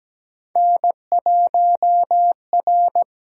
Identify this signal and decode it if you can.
Morse code, heard as a keyed tone, about 17 wpm, 700 Hz. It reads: N1R
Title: USA Call Signs - 4 Characters - 8x Character Spacing - 17wpm